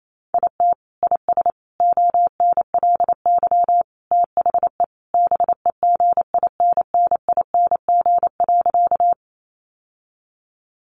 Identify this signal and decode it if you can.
Morse code, heard as a keyed tone, about 28 wpm, 700 Hz. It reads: IT IS ONLY THE BEGINNING.
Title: Sentences from Top 800 Words 28wpm